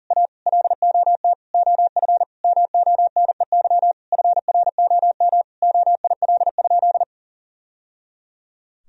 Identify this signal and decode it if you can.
Morse code, heard as a keyed tone, about 40 wpm, 700 Hz. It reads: A LOT OF MONEY FROM OIL?
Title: Sentences from Top 1000 Words - Review Part 4 - 40wpm